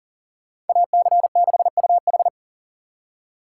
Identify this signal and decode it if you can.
Morse code, heard as a keyed tone, about 48 wpm, 700 Hz. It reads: AC6UH
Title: Speed Racing - USA Call Signs 40wpm